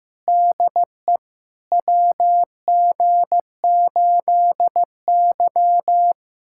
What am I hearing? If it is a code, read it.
Morse code, heard as a keyed tone, about 15 wpm, 700 Hz. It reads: DE WG8Y
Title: USA Callsign DE Callsign - 4 Characters - 15wpm